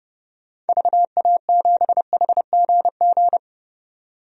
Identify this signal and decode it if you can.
Morse code, heard as a keyed tone, about 30 wpm, 700 Hz. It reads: VA7HGG
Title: DXCC-100 Call Signs - 6 Characters - 30wpm